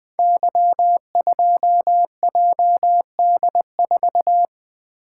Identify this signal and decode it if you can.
Morse code, heard as a keyed tone, about 20 wpm, 700 Hz. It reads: Y2JD4
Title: Code Groups - 5 Letter:Numbers - 20wpm